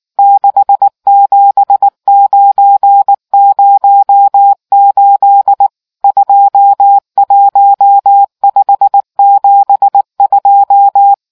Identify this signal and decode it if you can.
Morse code, heard as a keyed tone, about 19 wpm, 800 Hz. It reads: 67908 21572